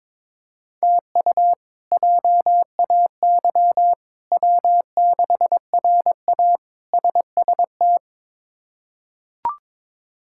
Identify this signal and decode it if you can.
Morse code, heard as a keyed tone, about 22 wpm, 700 Hz. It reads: TU JAY W6RA SST
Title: K1USN SST QSO Practice - 22wpm